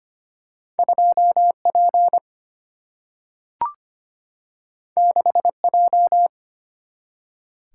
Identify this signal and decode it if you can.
Morse code, heard as a keyed tone, about 25 wpm, 700 Hz. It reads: 2P 6J
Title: Single-Letter Single-Number 25wpm